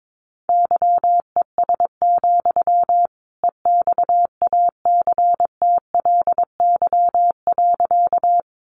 Morse code, heard as a keyed tone, about 22 wpm, 700 Hz. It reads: YES, EXACTLY.
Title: Sentences from Top 900 Words 22wpm